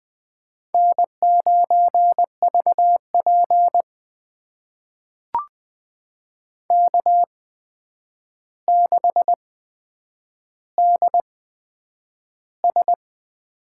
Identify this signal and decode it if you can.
Morse code, heard as a keyed tone, about 20 wpm, spaced as for 9 wpm, 700 Hz. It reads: N9VP K6DS
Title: USA Call Signs - 4 Characters - 8x Character Spacing - 20wpm